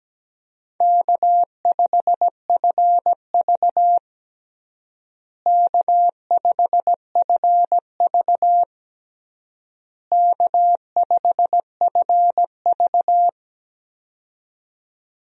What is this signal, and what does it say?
Morse code, heard as a keyed tone, about 17 wpm, 700 Hz. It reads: K5FV K5FV K5FV
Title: USA Call Signs - 4 Characters - Repeated 3x - 17wpm